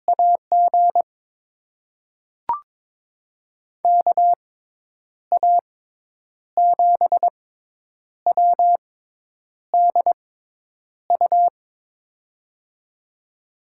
Morse code, heard as a keyed tone, about 21 wpm, spaced as for 8 wpm, 700 Hz. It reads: AG KA7WDU
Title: USA Call Signs - 6x Character Spacing - 22wpm